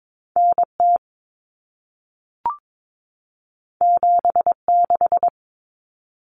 Morse code, heard as a keyed tone, about 22 wpm, 700 Hz. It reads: NT 76
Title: Common Cut Numbers - 2-Digit 22wpm